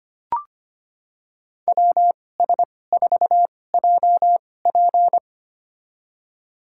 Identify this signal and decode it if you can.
Morse code, heard as a keyed tone, about 25 wpm, spaced as for 18 wpm, 700 Hz. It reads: WS4JP
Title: USA Call Signs - 5 Characters - 2x Character Spacing - 25wpm